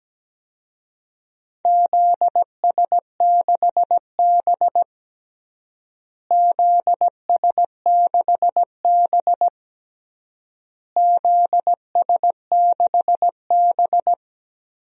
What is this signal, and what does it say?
Morse code, heard as a keyed tone, about 17 wpm, 700 Hz. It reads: ZS6B ZS6B ZS6B
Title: DXCC-100 Call Signs - 4 Characters - Repeated 3x - 17wpm